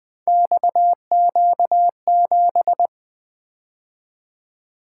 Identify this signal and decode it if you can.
Morse code, heard as a keyed tone, about 20 wpm, 700 Hz. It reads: XQ7
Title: Three Letter-Number Combinations - 20wpm